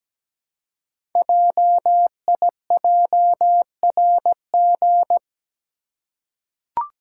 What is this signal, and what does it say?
Morse code, heard as a keyed tone, about 17 wpm, 700 Hz. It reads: JIJRG
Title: Code Groups - 5 Letters - 17wpm